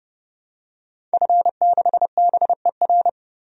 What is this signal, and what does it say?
Morse code, heard as a keyed tone, about 31 wpm, 700 Hz. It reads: F6BER
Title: DXCC-100 Call Signs - 5 Characters - 30wpm